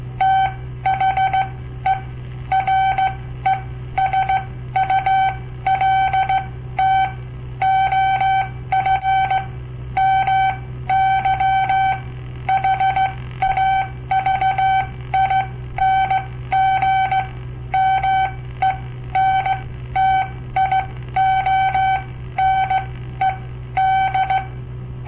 Morse code, heard as a keyed tone, about 16 wpm, spaced as for 13 wpm, 770 Hz. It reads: THE RESULT OF MY HAVING MENTIONED
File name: papertapes / local / Code Practice-01